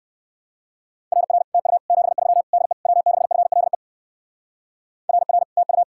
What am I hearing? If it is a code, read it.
Morse code, heard as a keyed tone, about 59 wpm, 700 Hz. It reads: RR TU 73 DE W6FLE RR TU